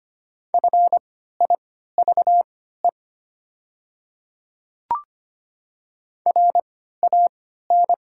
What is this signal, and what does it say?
Morse code, heard as a keyed tone, about 25 wpm, spaced as for 14 wpm, 700 Hz. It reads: FIVE RAN
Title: 3x Character Spacing - Top 500 Words - 25wpm